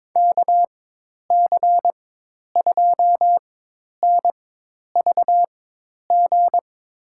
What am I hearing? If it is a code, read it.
Morse code, heard as a keyed tone, about 21 wpm, spaced as for 10 wpm, 700 Hz. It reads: KC2NVG
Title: USA Call Signs - 4x Character Spacing - 22wpm